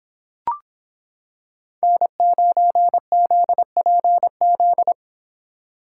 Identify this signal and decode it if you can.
Morse code, heard as a keyed tone, about 26 wpm, 700 Hz. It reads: N9ZPZ